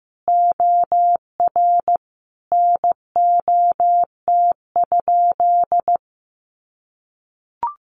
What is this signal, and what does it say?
Morse code, heard as a keyed tone, about 15 wpm, 700 Hz. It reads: OR NOT?
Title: Sentences from Top 600 Words 15wpm